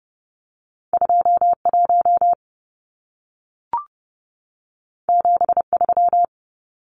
Morse code, heard as a keyed tone, about 30 wpm, 700 Hz. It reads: 21 73
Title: Numbers - 2-Digit 30wpm